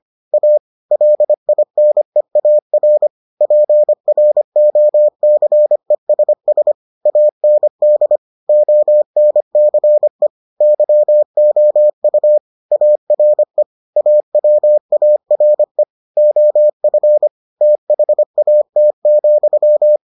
Morse code, heard as a keyed tone, about 25 wpm, 600 Hz. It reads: A LINEAR PROCESS AND ONCE YOU ARE AWARE OF THAT,